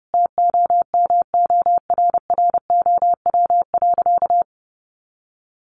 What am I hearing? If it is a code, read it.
Morse code, heard as a keyed tone, about 30 wpm, 700 Hz. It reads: TOMORROW.
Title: Additional Easy Phrases 30wpm